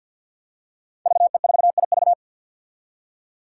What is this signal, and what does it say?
Morse code, heard as a keyed tone, about 52 wpm, 700 Hz. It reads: VE4IV
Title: Five Challenging Letter-Number Combinations 50wpm